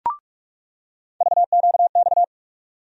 Morse code, heard as a keyed tone, about 44 wpm, 700 Hz. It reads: VQX
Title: USA Call Sign Postfixes - 3 Letters 45wpm